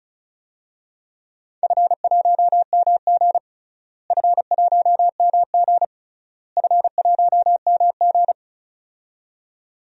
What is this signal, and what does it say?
Morse code, heard as a keyed tone, about 34 wpm, 700 Hz. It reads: F1MG F1MG F1MG
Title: DXCC-100 Call Signs - 4 Characters - Repeated 3x - 35wpm